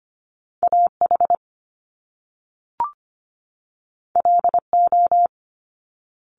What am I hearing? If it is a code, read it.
Morse code, heard as a keyed tone, about 25 wpm, 700 Hz. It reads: AH LO
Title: Top Two Letter Combinations in Words 25wpm